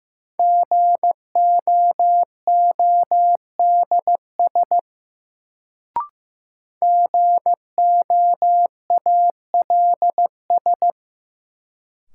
Morse code, heard as a keyed tone, about 15 wpm, 700 Hz. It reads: GOODS GOALS
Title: Sentences from Top 1000 Words - Review Part 6 - New Word Drill - 15wpm